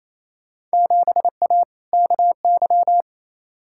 Morse code, heard as a keyed tone, about 28 wpm, 700 Hz. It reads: 7A KY
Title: Field Day Exchanges Encore - 28wpm